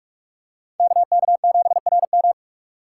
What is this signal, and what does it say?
Morse code, heard as a keyed tone, about 47 wpm, 700 Hz. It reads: KK7RM
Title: USA Call Signs - 5 Characters - 2x Character Spacing - 45wpm